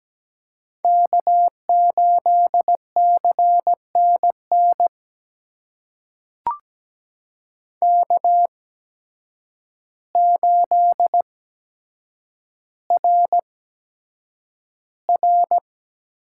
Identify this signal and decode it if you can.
Morse code, heard as a keyed tone, about 17 wpm, spaced as for 8 wpm, 700 Hz. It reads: K8CNN K8RR
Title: USA Call Signs - 8x Character Spacing - 17wpm